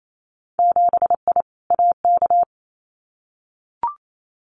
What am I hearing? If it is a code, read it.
Morse code, heard as a keyed tone, about 28 wpm, 700 Hz. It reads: 7I AK
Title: Winter Field Day Exchanges Encore 28wpm